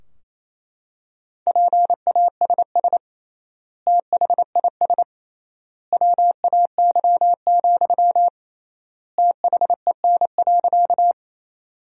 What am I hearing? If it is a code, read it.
Morse code, heard as a keyed tone, about 28 wpm, 700 Hz. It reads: PASS THIS WAY, THEN.
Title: 3x Word-Spacing - Sentences from Top 400 Words - 28wpm